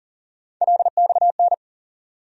Morse code, heard as a keyed tone, about 40 wpm, 700 Hz. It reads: LXN
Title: USA Call Sign Postfixes - 3 Letters 40wpm